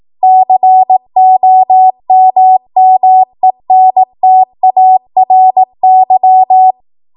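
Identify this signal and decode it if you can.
Morse code, heard as a keyed tone, about 18 wpm, 750 Hz. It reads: COMMENTARY
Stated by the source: ARRL 18 wpm Morse code practice archive transmission.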